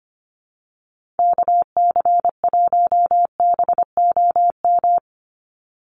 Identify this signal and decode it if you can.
Morse code, heard as a keyed tone, about 25 wpm, 700 Hz. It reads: KC1BOM
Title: USA Call Signs 25wpm